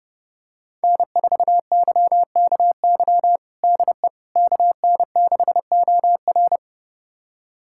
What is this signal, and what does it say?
Morse code, heard as a keyed tone, about 32 wpm, 700 Hz. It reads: N4YKY DE KN6OR
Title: USA Callsign DE Callsign 30wpm